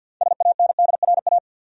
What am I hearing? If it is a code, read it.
Morse code, heard as a keyed tone, about 53 wpm, 700 Hz. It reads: SANDRA
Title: USA Common Names - 50wpm